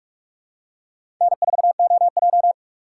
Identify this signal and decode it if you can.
Morse code, heard as a keyed tone, about 46 wpm, 700 Hz. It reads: N4OJ
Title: USA Call Signs - 4 Characters - 2x Character Spacing - 45wpm